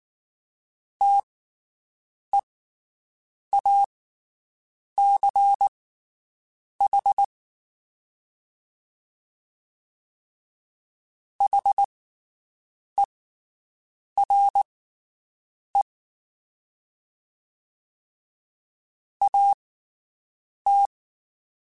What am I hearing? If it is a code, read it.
Morse code, heard as a keyed tone, about 19 wpm, spaced as for 7 wpm, 800 Hz. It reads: TEACH HERE AT
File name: (spaced) / QSOs Spaced-mp3s / WD0BGP 1 @18 sp